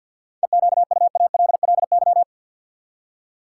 Morse code, heard as a keyed tone, about 52 wpm, 700 Hz. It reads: EQUALLY
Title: Seven Letter Words - 50wpm